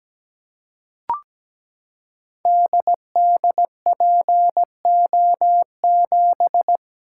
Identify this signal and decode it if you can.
Morse code, heard as a keyed tone, about 17 wpm, 690 Hz. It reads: DDPO7
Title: Code Groups - 5 Letter:Numbers - 17wpm